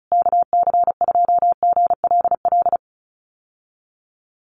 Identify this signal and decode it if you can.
Morse code, heard as a keyed tone, about 34 wpm, 700 Hz. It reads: KC2GLL
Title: USA Call Signs Repeated 3x 35wpm